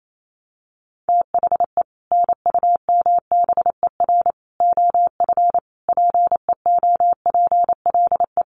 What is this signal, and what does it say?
Morse code, heard as a keyed tone, about 28 wpm, 700 Hz. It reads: THE NUMBER OF PEOPLE
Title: Sets of 5 Words Encore 28wpm